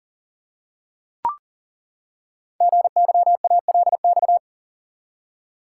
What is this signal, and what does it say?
Morse code, heard as a keyed tone, about 42 wpm, 690 Hz. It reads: GYALX
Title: Code Groups - 5 Letters - 40wpm